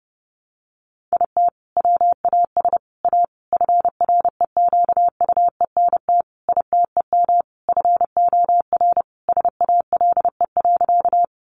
Morse code, heard as a keyed tone, about 30 wpm, 700 Hz. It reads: IT WAS A FREQUENT ITEM FOR SALE.
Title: Sentences from Top 1400 Words 30wpm